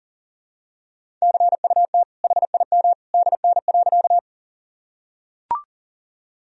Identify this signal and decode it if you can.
Morse code, heard as a keyed tone, about 37 wpm, 700 Hz. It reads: CUT HIM DN.
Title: Sentences from Top 300 Words and Abbreviations 40wpm